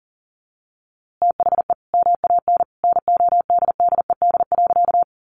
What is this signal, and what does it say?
Morse code, heard as a keyed tone, about 37 wpm, 700 Hz. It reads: THE MAN NODDED.
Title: A Sound Of Thunder 40wpm